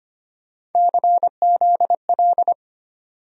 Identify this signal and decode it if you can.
Morse code, heard as a keyed tone, about 25 wpm, 700 Hz. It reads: CZL
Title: USA Call Sign Postfixes - 3 Letters 25wpm